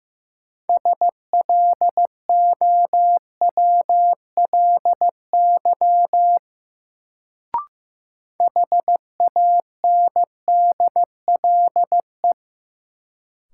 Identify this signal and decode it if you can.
Morse code, heard as a keyed tone, about 15 wpm, 700 Hz. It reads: SLOWLY HANDLE
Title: Sentences from Top 1100 Words - New Word Drill Alt - 15wpm